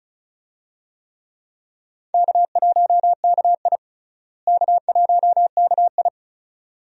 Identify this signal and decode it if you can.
Morse code, heard as a keyed tone, about 34 wpm, 700 Hz. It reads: K1KI K1KI
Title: USA Call Signs - 4 Characters - Repeated 2x - 35wpm